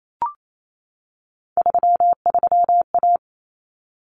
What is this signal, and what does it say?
Morse code, heard as a keyed tone, about 28 wpm, 700 Hz. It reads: 33A
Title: Common Cut Numbers - 3-Digit 28wpm